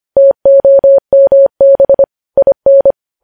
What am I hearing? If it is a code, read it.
Morse code, heard as a keyed tone, about 25 wpm, 550 Hz. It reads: TOMB IN
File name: podcast / News headlines-1x-25x25